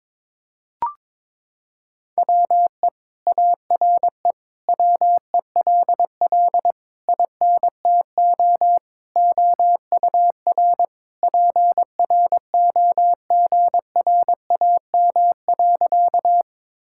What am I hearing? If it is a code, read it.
Morse code, heard as a keyed tone, about 21 wpm, 700 Hz. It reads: WE ARE WELL INTO OUR PROGRAM.
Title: Sentences from Top 1000 Words - Review Part 5 - 22wpm